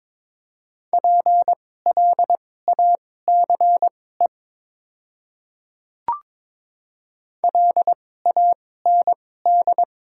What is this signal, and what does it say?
Morse code, heard as a keyed tone, about 22 wpm, spaced as for 16 wpm, 700 Hz. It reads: PLACE LAND
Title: 2x Character Spacing - Top 200 Words - 22wpm